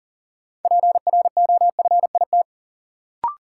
Morse code, heard as a keyed tone, about 40 wpm, 700 Hz. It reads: PROFIT
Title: Sentences from Top 800 Words - New Word Drill - 40wpm